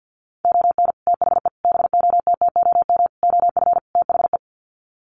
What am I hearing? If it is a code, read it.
Morse code, heard as a keyed tone, about 50 wpm, 700 Hz. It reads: ON THE BOTTOM OF THE
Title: Sets of 5 Words Encore2 50wpm